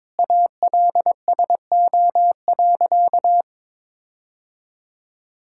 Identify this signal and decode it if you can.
Morse code, heard as a keyed tone, about 21 wpm, 700 Hz. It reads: ALSO.
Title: Sentences from Top 1600 Words 22wpm